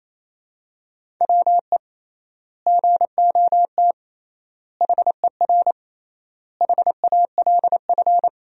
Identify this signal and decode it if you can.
Morse code, heard as a keyed tone, about 28 wpm, 700 Hz. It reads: WE GOT HER HALF
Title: 3x Word-Spacing - Sentences from Top 400 Words - 28wpm